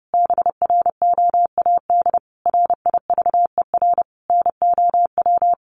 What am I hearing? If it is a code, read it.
Morse code, heard as a keyed tone, about 30 wpm, 700 Hz. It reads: BROAD RIVER NOW
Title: Sentences from Top 1000 Words 30wpm